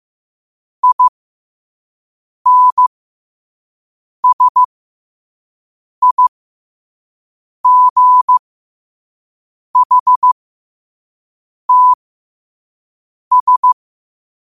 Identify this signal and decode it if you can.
Morse code, heard as a keyed tone, about 15 wpm, spaced as for 5 wpm, 1000 Hz. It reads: INSIGHTS
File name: qotdfiles / 5wpm